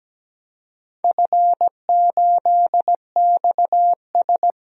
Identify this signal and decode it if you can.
Morse code, heard as a keyed tone, about 17 wpm, 700 Hz. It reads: F8XS
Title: DXCC-100 Call Signs - 4 Characters - 17wpm